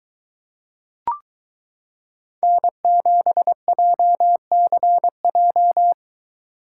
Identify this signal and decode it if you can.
Morse code, heard as a keyed tone, about 23 wpm, 700 Hz. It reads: N7JCJ